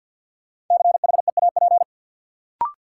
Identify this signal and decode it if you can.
Morse code, heard as a keyed tone, about 48 wpm, 700 Hz. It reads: CHEAP
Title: Sentences from Top 1200 Words - New Word Drill Alt - 50wpm